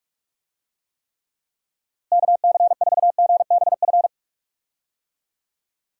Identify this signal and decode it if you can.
Morse code, heard as a keyed tone, about 41 wpm, 700 Hz. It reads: KC4GBF
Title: USA Call Signs - 6 Characters - Repeated 3x - 45wpm